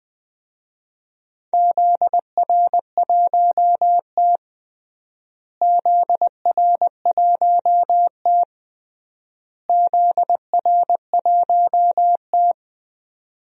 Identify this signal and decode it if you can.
Morse code, heard as a keyed tone, about 20 wpm, 700 Hz. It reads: ZR1T ZR1T ZR1T
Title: DXCC-100 Call Signs - 4 Characters - Repeated 3x - 20wpm